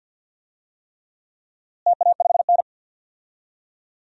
Morse code, heard as a keyed tone, about 50 wpm, 700 Hz. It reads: TA5N
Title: DXCC-100 Call Signs - 4 Characters - Repeated 3x - 50wpm